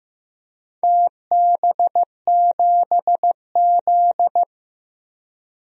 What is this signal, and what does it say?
Morse code, heard as a keyed tone, about 16 wpm, 700 Hz. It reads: TB7Z
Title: DXCC-100 Call Signs - 4 Characters - 15wpm